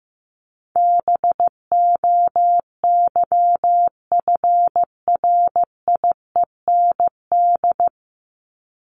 Morse code, heard as a keyed tone, about 15 wpm, 700 Hz. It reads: BOYFRIEND
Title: Top 501-750 Words 15wpm